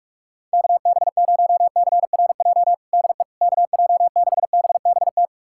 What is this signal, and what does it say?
Morse code, heard as a keyed tone, about 46 wpm, 700 Hz. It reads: KB0CRJ DE KJ6BBT
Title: USA Callsign DE Callsign - 6 Characters - 45wpm